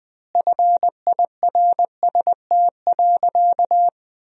Morse code, heard as a keyed tone, about 20 wpm, 700 Hz. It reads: FIRST.
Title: Sentences from Top 1000 Words - Review Part 1 - 20wpm